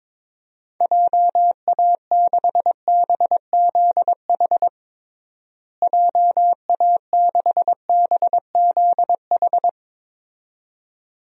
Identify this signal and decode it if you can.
Morse code, heard as a keyed tone, about 21 wpm, 700 Hz. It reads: JA6BZH JA6BZH
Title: DXCC-100 Call Signs - 6 Characters - Repeated 2x - 22wpm